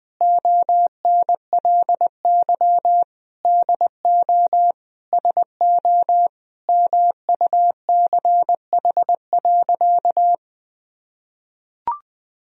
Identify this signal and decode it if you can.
Morse code, heard as a keyed tone, about 21 wpm, 700 Hz. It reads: ONLY DO SO MUCH.
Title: Sentences from Top 1000 Words - Review Part 3 - 20wpm